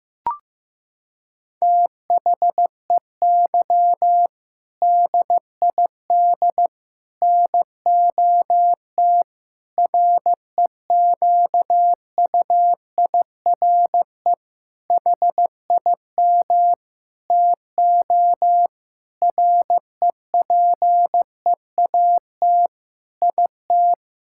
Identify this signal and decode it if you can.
Morse code, heard as a keyed tone, about 16 wpm, 700 Hz. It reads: THEY DID NOT REQUIRE HIM TO REPEAT IT
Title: Sentences from Top 1600 Words 15wpm